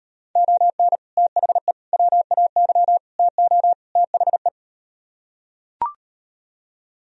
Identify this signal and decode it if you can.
Morse code, heard as a keyed tone, about 38 wpm, 700 Hz. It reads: ON THE WAY TO THE